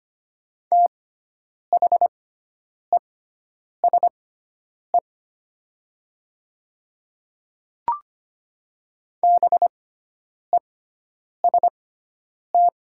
Morse code, heard as a keyed tone, about 25 wpm, spaced as for 9 wpm, 700 Hz. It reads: THESE BEST
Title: 6x Character Spacing - Top 500 Words - 25wpm